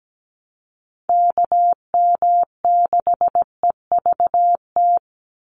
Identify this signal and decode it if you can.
Morse code, heard as a keyed tone, about 17 wpm, 700 Hz. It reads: KM6EVT
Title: USA Call Signs 17wpm